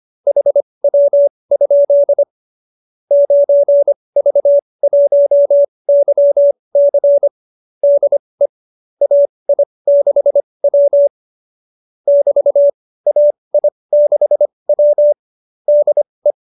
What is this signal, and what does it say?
Morse code, heard as a keyed tone, about 25 wpm, spaced as for 20 wpm, 590 Hz. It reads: HW? 9V1YC DE AI6W <BT> AI6W DE
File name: qso207b 20